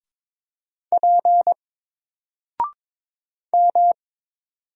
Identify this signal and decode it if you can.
Morse code, heard as a keyed tone, about 22 wpm, 700 Hz. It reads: P M
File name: Single-Letter-Number-22wpm